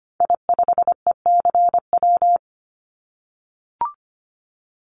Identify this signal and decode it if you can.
Morse code, heard as a keyed tone, about 25 wpm, 700 Hz. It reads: I5ECW
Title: Call Signs - 25wpm